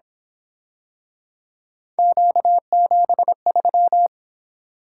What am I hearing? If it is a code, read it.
Morse code, heard as a keyed tone, about 28 wpm, 700 Hz. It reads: Q73